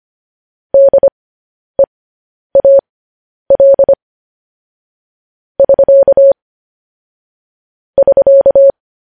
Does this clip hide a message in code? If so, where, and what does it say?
Yes: Morse code from 0.7 to 8.7 s, heard as a keyed tone, about 25 wpm, spaced as for 10 wpm, 550 Hz. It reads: DEAL <SK> <SK>